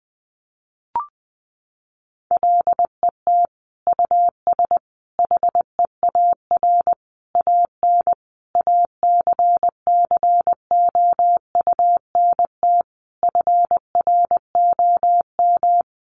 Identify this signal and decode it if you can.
Morse code, heard as a keyed tone, about 20 wpm, 700 Hz. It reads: LET US HEAR AN ACCOUNT FROM
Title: Sentences from Top 900 Words 20wpm